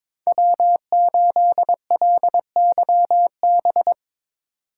Morse code, heard as a keyed tone, about 22 wpm, 700 Hz. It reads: W8LYB
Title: USA Call Signs - 8x Character Spacing - 22wpm